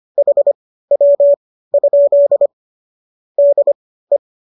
Morse code, heard as a keyed tone, about 25 wpm, spaced as for 15 wpm, 570 Hz. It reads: HW? DE